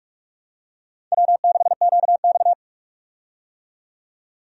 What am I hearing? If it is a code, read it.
Morse code, heard as a keyed tone, about 46 wpm, 700 Hz. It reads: W6QX
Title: USA Call Signs - 4 Characters - 45wpm